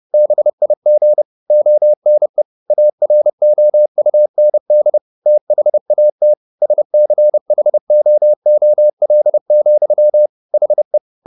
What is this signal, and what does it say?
Morse code, heard as a keyed tone, about 30 wpm, 600 Hz. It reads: BIG ONE AROUND THAT SCHOOL, HE
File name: the old man and the sea / the old man and the sea 1 30wpm